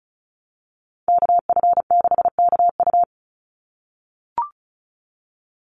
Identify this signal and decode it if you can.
Morse code, heard as a keyed tone, about 34 wpm, 700 Hz. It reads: KF6KU
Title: USA Call Signs 35wpm